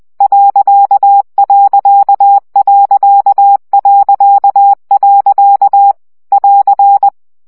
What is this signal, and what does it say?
Morse code, heard as a keyed tone, about 20 wpm, 800 Hz. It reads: ..... <AR>
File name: CW gyakorlas pont 17 baud